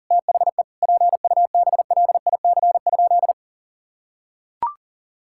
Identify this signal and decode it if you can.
Morse code, heard as a keyed tone, about 40 wpm, 700 Hz. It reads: THE PUBLIC?
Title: Sentences from Top 1000 Words - Review Part 5 - 40wpm